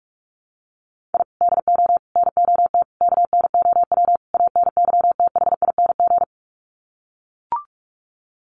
Morse code, heard as a keyed tone, about 44 wpm, 700 Hz. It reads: I DO NOT KNOW ANYTHING
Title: Sets of 5 Words Encore 45wpm